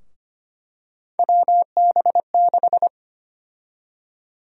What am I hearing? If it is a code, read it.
Morse code, heard as a keyed tone, about 25 wpm, 700 Hz. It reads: WB6
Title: USA Call Sign Prefixes - Repeated 2x - 25wpm